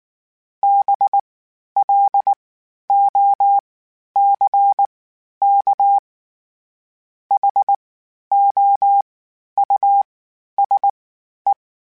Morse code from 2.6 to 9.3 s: O C K H O

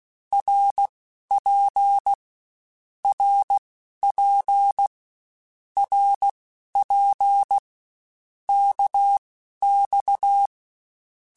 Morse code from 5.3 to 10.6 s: RP KX

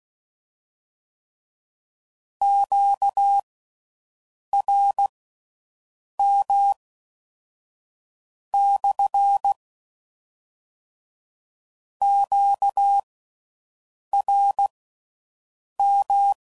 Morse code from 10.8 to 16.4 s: QRM

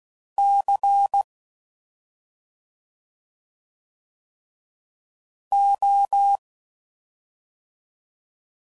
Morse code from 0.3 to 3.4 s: C

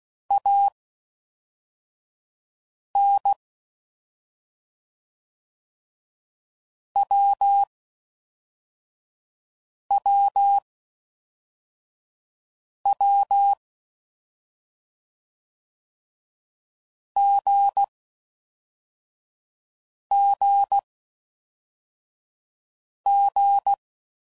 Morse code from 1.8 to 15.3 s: N W W W

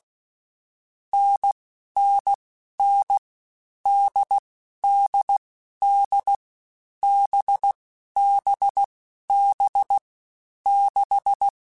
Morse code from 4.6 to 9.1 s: D D B B

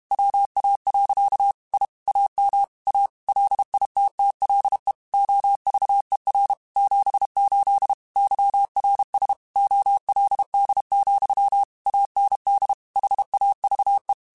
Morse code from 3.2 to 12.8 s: LITTLE OVER 78 YRS OLD, AND

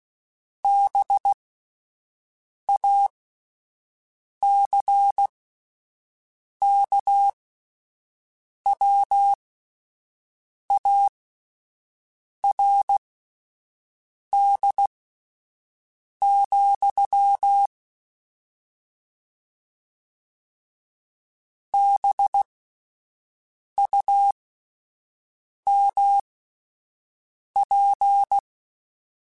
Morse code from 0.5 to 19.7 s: BACKWARD,

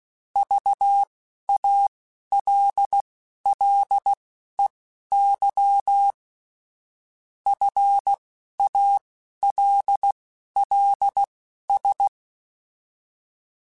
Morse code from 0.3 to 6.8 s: VALLEY

